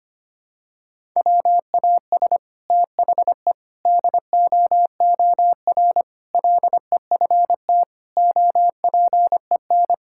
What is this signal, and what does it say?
Morse code, heard as a keyed tone, about 25 wpm, 700 Hz. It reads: WAS THE DOOR LEFT OPEN
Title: Sentences from Top 1000 Words - Review Part 6 - 25wpm